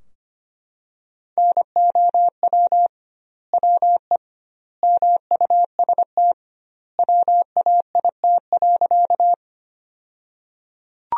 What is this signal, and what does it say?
Morse code, heard as a keyed tone, about 25 wpm, 700 Hz. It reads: NOW WE MUST WAIT.
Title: 2x Word-Spacing - Sentences from Top 500 Words - 25wpm